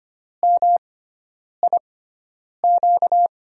Morse code, heard as a keyed tone, about 25 wpm, spaced as for 9 wpm, 700 Hz. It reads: MIQ
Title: USA Call Signs - 5 Characters - 6x Character Spacing - 25wpm